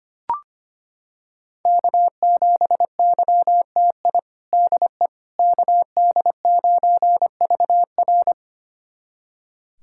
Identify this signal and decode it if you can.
Morse code, heard as a keyed tone, about 25 wpm, 700 Hz. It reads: K7YTI DE KD9VR
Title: USA Callsign DE Callsign - 5 Characters - 25wpm